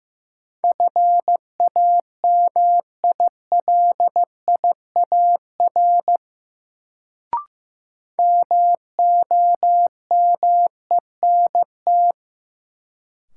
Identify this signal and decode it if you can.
Morse code, heard as a keyed tone, about 16 wpm, 700 Hz. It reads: FAMILIAR MOMENT
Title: Sentences from Top 1000 Words - New Word Drill - 15wpm